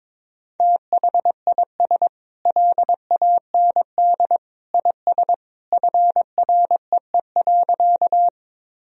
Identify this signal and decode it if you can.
Morse code, heard as a keyed tone, about 22 wpm, 700 Hz. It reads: THIS LAND IS FREE.
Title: Sentences from Top 1000 Words - Review Part 1 - 22wpm